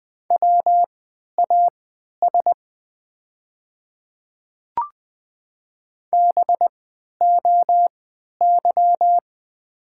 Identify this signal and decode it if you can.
Morse code, heard as a keyed tone, about 20 wpm, spaced as for 11 wpm, 700 Hz. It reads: WAS BOY
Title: 3x Character Spacing - Top 300 Words - 20wpm